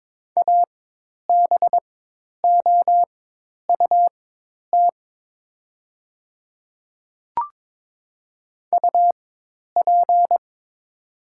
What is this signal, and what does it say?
Morse code, heard as a keyed tone, about 22 wpm, spaced as for 10 wpm, 700 Hz. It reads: ABOUT UP
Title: 4x Character Spacing - Top 100 Words - 22wpm